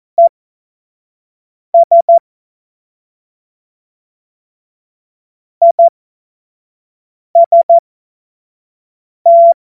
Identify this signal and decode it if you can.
Morse code, heard as a keyed tone, about 14 wpm, spaced as for 5 wpm, 680 Hz. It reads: ES IST
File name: Morsekurs / Witze / Witz 15. Juni 2021, 25 F70